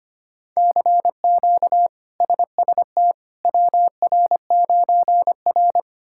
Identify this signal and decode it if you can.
Morse code, heard as a keyed tone, about 25 wpm, 700 Hz. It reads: CQ SST WR9R
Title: K1USN SST QSO Practice - 25wpm